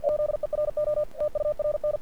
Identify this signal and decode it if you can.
Morse code, heard as a keyed tone, about 48 wpm, 600 Hz. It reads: ZERO TURN
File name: archived / CRSnet